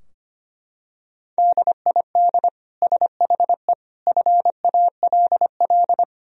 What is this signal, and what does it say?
Morse code, heard as a keyed tone, about 25 wpm, 700 Hz. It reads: DID SHE FALL
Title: Sentences from Top 1000 Words - Review Part 2 - 25wpm